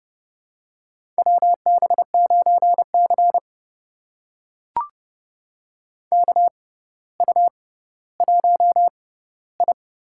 Morse code, heard as a keyed tone, about 32 wpm, spaced as for 16 wpm, 700 Hz. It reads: WB9C KU1I
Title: USA Call Signs - 6x Character Spacing - 30wpm